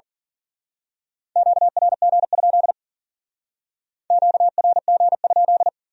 Morse code, heard as a keyed tone, about 44 wpm, 700 Hz. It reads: QRG? QRG?
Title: Speed-Racing Q-Codes - 40wpm